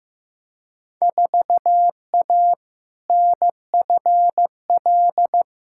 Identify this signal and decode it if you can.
Morse code, heard as a keyed tone, about 15 wpm, 700 Hz. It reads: 4A NFL
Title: Field Day Exchanges - 15wpm